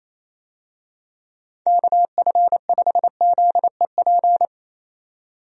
Morse code, heard as a keyed tone, about 28 wpm, 700 Hz. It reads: KF5ZEP
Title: USA Call Signs - 6 Characters - 28wpm